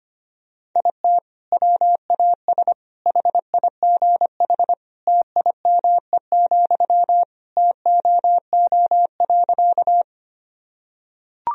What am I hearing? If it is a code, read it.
Morse code, heard as a keyed tone, about 25 wpm, 700 Hz. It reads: IT WAS HIGH TIME, TOO.
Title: Sentences from Top 1000 Words - Review Part 1 - 25wpm